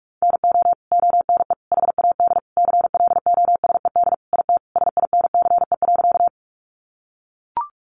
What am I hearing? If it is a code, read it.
Morse code, heard as a keyed tone, about 44 wpm, 700 Hz. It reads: NO ONE HAD CLOSED IT SINCE.
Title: Sentences from Top 1400 Words 45wpm